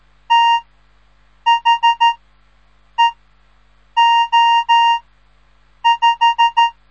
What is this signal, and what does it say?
Morse code, heard as a keyed tone, about 13 wpm, spaced as for 7 wpm, 930 Hz. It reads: THEO5
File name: radiomorningstar / morse / les 3